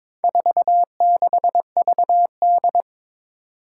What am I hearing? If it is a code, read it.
Morse code, heard as a keyed tone, about 22 wpm, 700 Hz. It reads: 46VD
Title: Four Challenging Letter-Number Combinations 22wpm